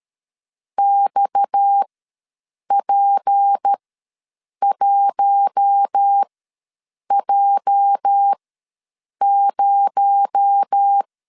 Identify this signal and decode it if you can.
Morse code, heard as a keyed tone, about 13 wpm, spaced as for 7 wpm, 800 Hz. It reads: XP1J0
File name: LEZIONI / 10 SIGNS / 10B INT ws 12wpm 3527